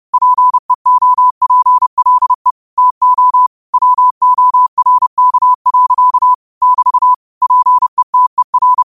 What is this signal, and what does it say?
Morse code, heard as a keyed tone, about 30 wpm, 1000 Hz. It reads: PEOPLE TO WORK. <BT> PETER
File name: qotdfiles / 30wpm